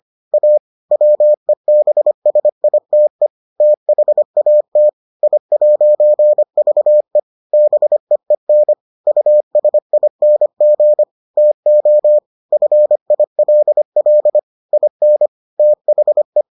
Morse code, heard as a keyed tone, about 25 wpm, 600 Hz. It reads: A WEBSITE THAT I'VE BEEN USING TO FILL IN THE